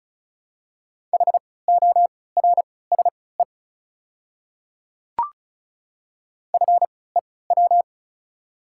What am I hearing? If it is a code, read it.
Morse code, heard as a keyed tone, about 36 wpm, spaced as for 20 wpm, 700 Hz. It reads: HORSE FEW
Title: 3x Character Spacing - Top 500 Words - 35wpm